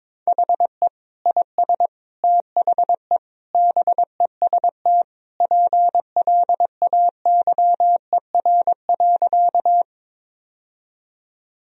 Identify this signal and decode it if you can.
Morse code, heard as a keyed tone, about 22 wpm, 700 Hz. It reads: HE IS THE BEST PLAYER.
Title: Sentences from Top 1000 Words - Review Part 4 - 22wpm